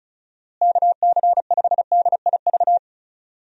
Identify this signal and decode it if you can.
Morse code, heard as a keyed tone, about 34 wpm, 700 Hz. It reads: KC5DIV
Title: USA Call Signs - 6 Characters - Repeated 2x - 35wpm